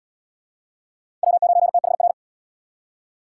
Morse code, heard as a keyed tone, about 77 wpm, 700 Hz. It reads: W9THR